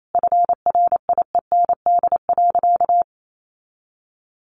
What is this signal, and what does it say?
Morse code, heard as a keyed tone, about 28 wpm, 700 Hz. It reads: FRIEND.
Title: Sentences from Top 1400 Words 28wpm